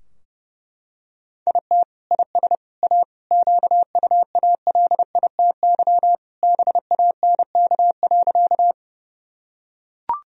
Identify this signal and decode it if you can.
Morse code, heard as a keyed tone, about 30 wpm, 700 Hz. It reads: IT IS A QUALITY BANK.
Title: Sentences from Top 1000 Words - Review Part 4 - 30wpm